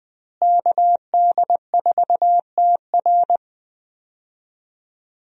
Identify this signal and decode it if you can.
Morse code, heard as a keyed tone, about 20 wpm, 700 Hz. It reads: KD4TR
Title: USA Call Signs - 5 Characters - Repeated 3x - 20wpm